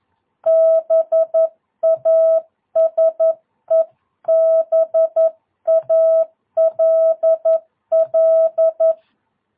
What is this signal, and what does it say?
Morse code, heard as a keyed tone, about 11 wpm, 650 Hz. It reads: BASEBALL
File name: jh8cbh / 2022cw